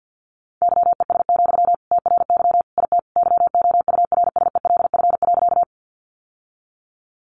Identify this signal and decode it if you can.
Morse code, heard as a keyed tone, about 50 wpm, 700 Hz. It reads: YES, TRY IT YOURSELF.
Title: Sentences from Top 600 Words 50wpm